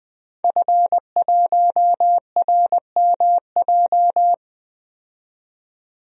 Morse code, heard as a keyed tone, about 20 wpm, 700 Hz. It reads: F1RMJ
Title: DXCC-100 Call Signs - 5 Characters - 20wpm